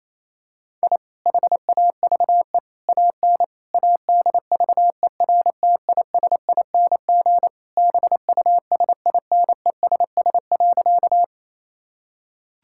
Morse code, heard as a keyed tone, about 28 wpm, 700 Hz. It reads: I HAVE AN ADVERTISING BUSINESS.
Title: Sentences from Top 1000 Words - Review Part 6 - 28wpm